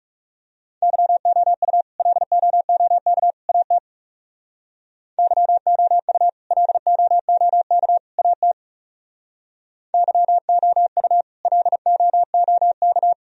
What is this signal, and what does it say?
Morse code, heard as a keyed tone, about 37 wpm, 700 Hz. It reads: YOU LOOK AT YOU LOOK AT YOU LOOK
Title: Speed-Racing - Sets of 3 Words 30wpm